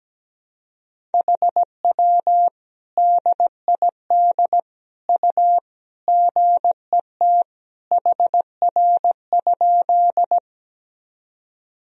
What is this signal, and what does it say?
Morse code, heard as a keyed tone, about 17 wpm, 700 Hz. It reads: HW DID U GET HR?
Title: Sentences from Top 200 Words and Abbreviations 17wpm